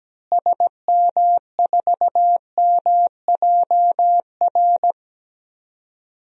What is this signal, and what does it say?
Morse code, heard as a keyed tone, about 17 wpm, 700 Hz. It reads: SM4MJR
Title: DXCC-100 Call Signs - 6 Characters - Repeated 2x - 17wpm